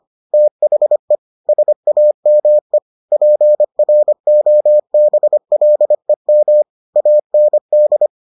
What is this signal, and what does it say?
Morse code, heard as a keyed tone, about 25 wpm, 600 Hz. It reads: THE SAME PROBLEM AND